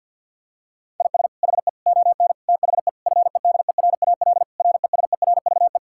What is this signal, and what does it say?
Morse code, heard as a keyed tone, about 50 wpm, 700 Hz. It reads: IS HE ON THE FEDERAL RESERVE
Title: Sentences from Top 1000 Words - Review Part 6 - 50wpm